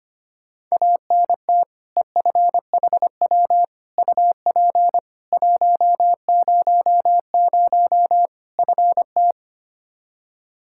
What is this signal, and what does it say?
Morse code, heard as a keyed tone, about 25 wpm, 700 Hz. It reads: ANT EFHW UP 100 FT
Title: Practice QSO - Exchange 2 - Info - 25wpm